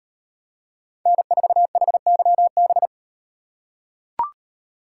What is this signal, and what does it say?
Morse code, heard as a keyed tone, about 38 wpm, 700 Hz. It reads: N4HYB